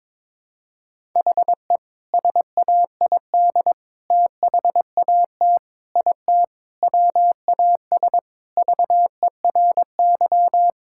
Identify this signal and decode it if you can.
Morse code, heard as a keyed tone, about 22 wpm, 700 Hz. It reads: HE SAID THAT IT WAS VERY
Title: Sentences from Top 1000 Words - Review Part 6 - 22wpm